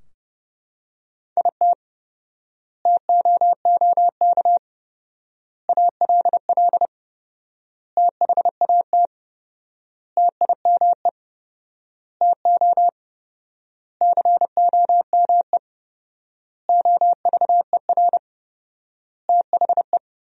Morse code, heard as a keyed tone, about 30 wpm, 700 Hz. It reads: IT TOOK ALL THAT TIME TO COME OVER THE
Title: 4x Word-Spacing - Sentences from Top 300 Words - 30wpm